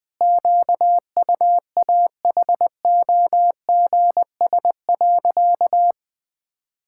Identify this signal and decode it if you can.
Morse code, heard as a keyed tone, about 20 wpm, 700 Hz. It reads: QUAHOGS.
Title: Pangram Challenge v2 - 20wpm